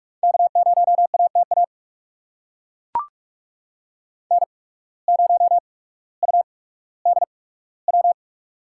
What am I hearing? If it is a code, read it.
Morse code, heard as a keyed tone, about 42 wpm, spaced as for 19 wpm, 700 Hz. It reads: K0ATA N0UDW
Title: USA Call Signs - 5 Characters - 8x Character Spacing - 45wpm